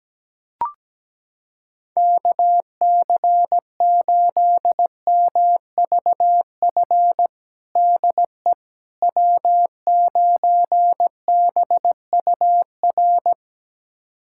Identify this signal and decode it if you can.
Morse code, heard as a keyed tone, about 17 wpm, 700 Hz. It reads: KC8MVF DE W9BUR
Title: USA Callsign DE Callsign 17wpm